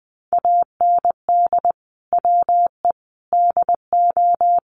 Morse code, heard as a keyed tone, about 20 wpm, 700 Hz. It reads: AND WE DO
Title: Sets of 4 Words 20wpm